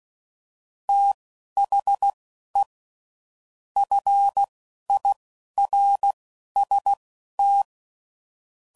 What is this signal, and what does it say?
Morse code, heard as a keyed tone, about 16 wpm, spaced as for 11 wpm, 800 Hz. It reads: THE FIRST
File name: CW Code Groups @10WPM / Technical Subjects / SwPwrSup